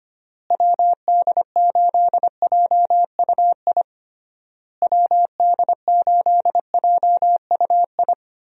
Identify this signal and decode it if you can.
Morse code, heard as a keyed tone, about 25 wpm, 700 Hz. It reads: WD8JUI WD8JUI
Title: USA Call Signs - 6 Characters - Repeated 3x - 25wpm